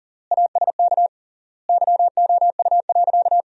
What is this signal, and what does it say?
Morse code, heard as a keyed tone, about 38 wpm, 700 Hz. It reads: ASK YOU.
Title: 3x Word-Spacing - Sentences from Top 400 Words - 40wpm